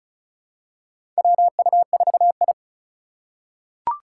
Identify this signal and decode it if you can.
Morse code, heard as a keyed tone, about 36 wpm, 700 Hz. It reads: WU4I
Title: USA Call Signs - 4 Characters - 3x Character Spacing - 35wpm